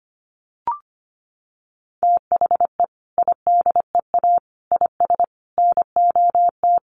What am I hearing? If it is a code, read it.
Morse code, heard as a keyed tone, about 25 wpm, 700 Hz. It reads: THE IDEA IS NOT
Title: Sentences from Top 1300 Words 25wpm